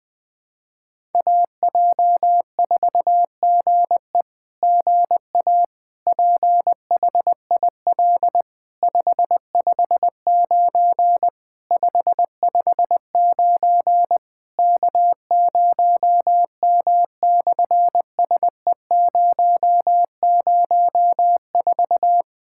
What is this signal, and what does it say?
Morse code, heard as a keyed tone, about 20 wpm, 700 Hz. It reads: AJ4GE GA PHIL 559 559 K0M/SE004
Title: SOTA QSO Practice 20wpm